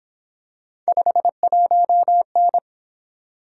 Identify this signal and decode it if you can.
Morse code, heard as a keyed tone, about 26 wpm, 700 Hz. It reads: 51N
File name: Speed-Racing - Three Letter-Number Combinations - 22wpm